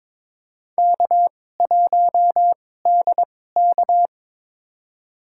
Morse code, heard as a keyed tone, about 22 wpm, spaced as for 16 wpm, 700 Hz. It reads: K1DK
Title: USA Call Signs - 4 Characters - 2x Character Spacing - 22wpm